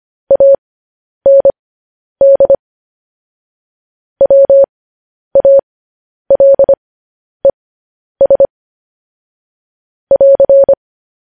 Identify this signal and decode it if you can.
Morse code, heard as a keyed tone, about 25 wpm, spaced as for 10 wpm, 550 Hz. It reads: AND WALES <AR>
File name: podcast / News headlines-1x-25x10